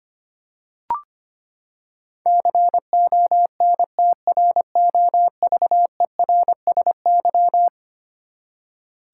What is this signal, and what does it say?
Morse code, heard as a keyed tone, about 25 wpm, 700 Hz. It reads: CONTROVERSY
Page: morse-code-ninja-practice-25wpm